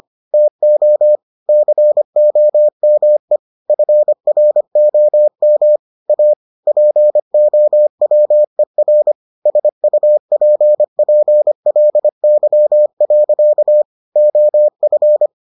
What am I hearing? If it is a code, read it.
Morse code, heard as a keyed tone, about 25 wpm, 600 Hz. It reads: TO COME FROM A POWER SUPPLY. OF